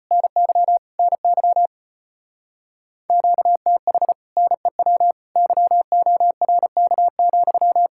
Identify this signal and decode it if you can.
Morse code, heard as a keyed tone, about 33 wpm, 700 Hz. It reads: NY NY QTH NEW YORK,